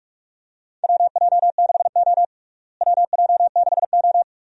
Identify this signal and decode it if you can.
Morse code, heard as a keyed tone, about 44 wpm, 700 Hz. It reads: WJ6O WJ6O
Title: USA Call Signs - 4 Characters - Repeated 2x - 45wpm